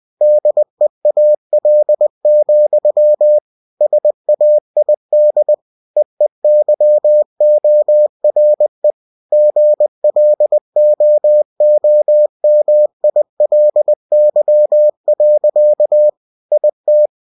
Morse code, heard as a keyed tone, about 20 wpm, 600 Hz. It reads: DEAL, SAID EEYORE GLOOMILY. IT
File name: cw-books / winnie-the-pooh / winnie-20wpm-page-0027